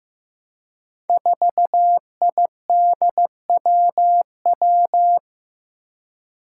Morse code, heard as a keyed tone, about 15 wpm, 700 Hz. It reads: 4IDWW
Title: Five Challenging Letter-Number Combinations 15wpm